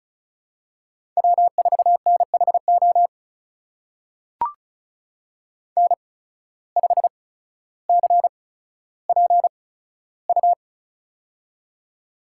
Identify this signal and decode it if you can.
Morse code, heard as a keyed tone, about 36 wpm, spaced as for 15 wpm, 700 Hz. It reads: W4NHO N5CPU
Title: USA Call Signs - 8x Character Spacing - 35wpm